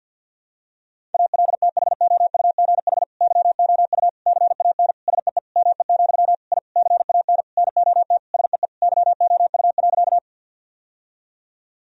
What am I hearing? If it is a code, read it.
Morse code, heard as a keyed tone, about 51 wpm, 700 Hz. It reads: ALTHOUGH YOU CAN SEE ME, I CAN NOT SEE YOU.
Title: Sentences from Top 1000 Words - Review Part 3 - 50wpm